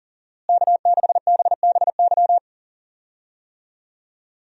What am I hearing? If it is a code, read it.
Morse code, heard as a keyed tone, about 40 wpm, 700 Hz. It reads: K6BBY
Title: USA Call Signs - 5 Characters - Repeated 2x - 40wpm